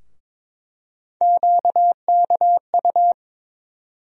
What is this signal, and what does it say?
Morse code, heard as a keyed tone, about 22 wpm, 700 Hz. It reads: QKU
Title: USA Call Sign Postfixes - 3 Letters 22wpm